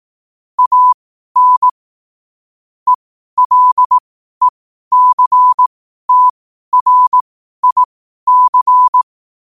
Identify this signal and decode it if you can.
Morse code, heard as a keyed tone, about 18 wpm, spaced as for 12 wpm, 1000 Hz. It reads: AN ELECTRIC
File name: qotdfiles / 12wpm